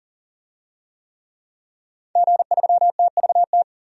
Morse code, heard as a keyed tone, about 40 wpm, 700 Hz. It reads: G3TVT
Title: DXCC-100 Call Signs - 5 Characters - 40wpm